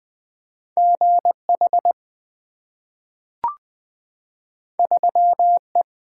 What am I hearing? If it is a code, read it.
Morse code, heard as a keyed tone, about 20 wpm, 700 Hz. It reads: GH 3E
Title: Two Letter-Number Combinations - 20wpm